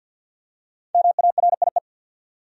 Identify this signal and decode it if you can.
Morse code, heard as a keyed tone, about 48 wpm, 700 Hz. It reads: MARIE
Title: USA Common Names - 50wpm